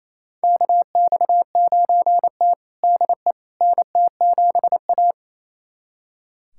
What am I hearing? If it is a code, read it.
Morse code, heard as a keyed tone, about 28 wpm, 700 Hz. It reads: KX9T DE NT7A
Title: USA Callsign DE Callsign - 4 Characters - 28wpm